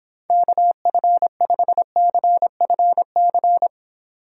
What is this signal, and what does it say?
Morse code, heard as a keyed tone, about 31 wpm, 700 Hz. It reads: KF5CFC